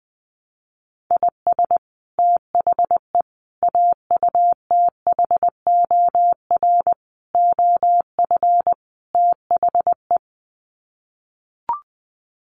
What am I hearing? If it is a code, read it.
Morse code, heard as a keyed tone, about 20 wpm, 700 Hz. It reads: IS THE AUTHOR OF THE
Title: Sets of 5 Words Encore 20wpm